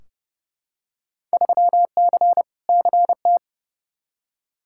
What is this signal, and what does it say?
Morse code, heard as a keyed tone, about 30 wpm, 700 Hz. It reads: 3C CT
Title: Field Day Exchanges Encore - 30wpm